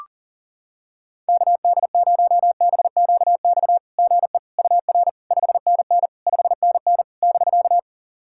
Morse code, heard as a keyed tone, about 44 wpm, 700 Hz. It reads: KD0BQX GE UR 5NN 5NN <BK>
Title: SOTA QSO Practice 40wpm